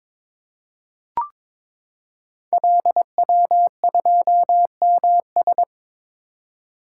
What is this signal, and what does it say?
Morse code, heard as a keyed tone, about 22 wpm, 700 Hz. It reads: LW2MS
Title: Code Groups - 5 Letter:Numbers - 22wpm